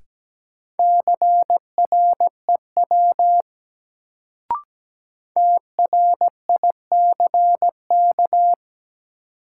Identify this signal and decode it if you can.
Morse code, heard as a keyed tone, about 17 wpm, 700 Hz. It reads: CREW TRICK
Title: Sentences from Top 1500 Words - New Word Drill Alt - 17wpm